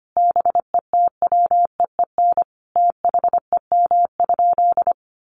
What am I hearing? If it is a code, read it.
Morse code, heard as a keyed tone, about 25 wpm, 700 Hz. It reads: BETWEEN THEM?
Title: Sentences from Top 1300 Words 25wpm